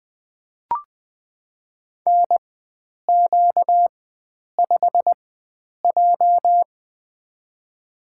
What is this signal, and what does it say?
Morse code, heard as a keyed tone, about 20 wpm, spaced as for 9 wpm, 700 Hz. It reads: NQ5J
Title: USA Call Signs - 4 Characters - 4x Character Spacing - 20wpm